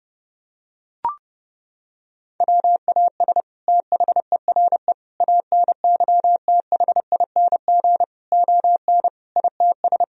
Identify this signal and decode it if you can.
Morse code, heard as a keyed tone, about 30 wpm, 700 Hz. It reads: WAS THERE ANYTHING ON ITS
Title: Sentences from Top 1000 Words - Review Part 3 - 30wpm